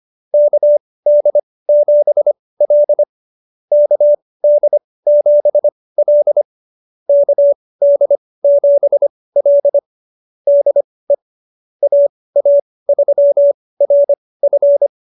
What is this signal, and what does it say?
Morse code, heard as a keyed tone, about 25 wpm, spaced as for 18 wpm, 590 Hz. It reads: KD7L KD7L KD7L DE AA3RF
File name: qso222 18